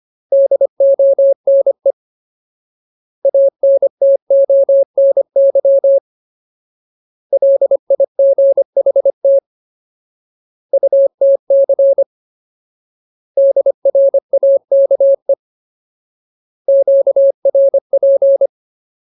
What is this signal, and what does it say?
Morse code, heard as a keyed tone, about 25 wpm, 550 Hz. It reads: DONE ANTONY LIGHT UTC DRAKE QRP
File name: Ultimate Word List / 25WPM Ultimate